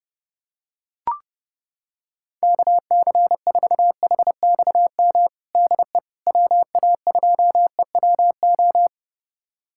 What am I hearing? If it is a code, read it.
Morse code, heard as a keyed tone, about 30 wpm, 700 Hz. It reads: KC4HXM DE WA2EWO
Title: USA Callsign DE Callsign 30wpm